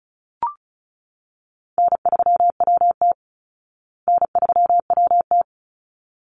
Morse code, heard as a keyed tone, about 34 wpm, 700 Hz. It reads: N3WT N3WT
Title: Call Signs Repeated 3x - 35wpm